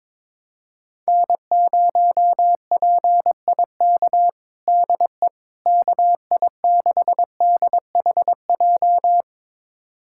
Morse code, heard as a keyed tone, about 22 wpm, 700 Hz. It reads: N0PIK DE KI6DHJ
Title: USA Callsign DE Callsign 22wpm